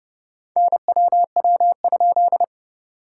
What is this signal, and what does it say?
Morse code, heard as a keyed tone, about 31 wpm, 690 Hz. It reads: NWW?
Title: SOTA QSO Practice 30wpm